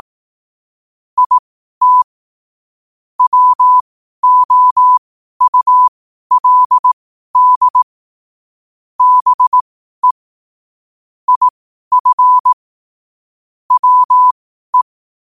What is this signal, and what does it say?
Morse code, heard as a keyed tone, about 18 wpm, spaced as for 12 wpm, 1000 Hz. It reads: IT WOULD BE IF WE
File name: qotdfiles / 12wpm